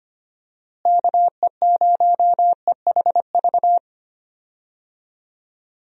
Morse code, heard as a keyed tone, about 25 wpm, 700 Hz. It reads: KE0EHV
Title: USA Call Signs - 6 Characters - Repeated 2x - 25wpm